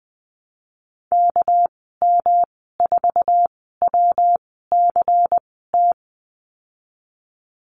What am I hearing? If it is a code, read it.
Morse code, heard as a keyed tone, about 20 wpm, 700 Hz. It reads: K M 4 W C T
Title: 231 Lesson 37g - Letter x - Call Signs 20wpm